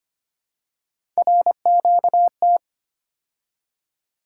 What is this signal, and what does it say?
Morse code, heard as a keyed tone, about 25 wpm, 700 Hz. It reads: RQT
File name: Three Letter-Number Combinations - 25wpm